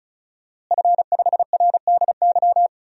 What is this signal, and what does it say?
Morse code, heard as a keyed tone, about 36 wpm, 700 Hz. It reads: F5RDY
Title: DXCC-100 Call Signs - 5 Characters - Repeated 3x - 35wpm